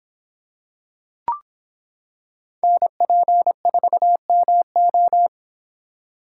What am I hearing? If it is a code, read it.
Morse code, heard as a keyed tone, about 26 wpm, 700 Hz. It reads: NP4MO